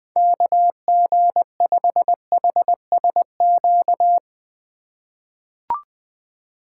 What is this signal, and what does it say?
Morse code, heard as a keyed tone, about 21 wpm, 700 Hz. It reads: KG5HSQ
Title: USA Call Signs - 6 Characters - 4x Character Spacing - 20wpm